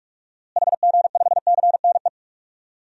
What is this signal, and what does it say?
Morse code, heard as a keyed tone, about 41 wpm, 700 Hz. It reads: HG5CNE
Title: DXCC-100 Call Signs - 6 Characters - Repeated 2x - 45wpm